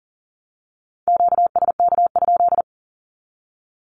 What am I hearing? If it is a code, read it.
Morse code, heard as a keyed tone, about 40 wpm, 700 Hz. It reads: QSK?
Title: Q-Codes Repeated 3x 40wpm